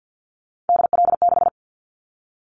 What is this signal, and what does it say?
Morse code, heard as a keyed tone, about 50 wpm, 700 Hz. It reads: DL6
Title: Callsign Prefixes for Top 50 DXCC - 50wpm